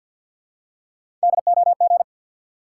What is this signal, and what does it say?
Morse code, heard as a keyed tone, about 50 wpm, 700 Hz. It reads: DOG
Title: Three Letter Words - 50wpm